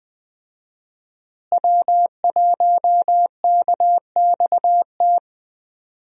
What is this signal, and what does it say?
Morse code, heard as a keyed tone, about 21 wpm, 700 Hz. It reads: W1KXT
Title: USA Call Signs - 5 Characters - Repeated 2x - 20wpm